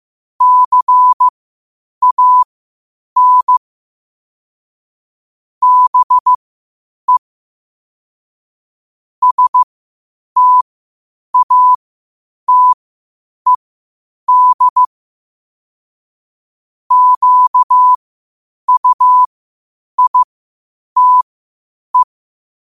Morse code, heard as a keyed tone, about 15 wpm, spaced as for 8 wpm, 1000 Hz. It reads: CAN BE STATED QUITE
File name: qotdfiles / 8wpm